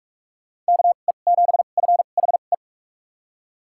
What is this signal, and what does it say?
Morse code, heard as a keyed tone, about 43 wpm, spaced as for 32 wpm, 700 Hz. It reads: KE7FHE
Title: USA Call Signs - 2x Character Spacing - 45wpm